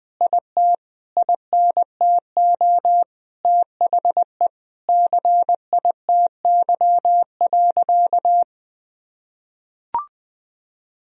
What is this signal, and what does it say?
Morse code, heard as a keyed tone, about 20 wpm, 700 Hz. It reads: IT INTO THE CITY.
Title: Sentences from Top 1000 Words - Review Part 1 - 20wpm